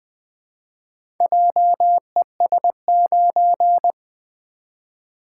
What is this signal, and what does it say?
Morse code, heard as a keyed tone, about 20 wpm, 700 Hz. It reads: JES9
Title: Four Letter-Number Combinations - 20wpm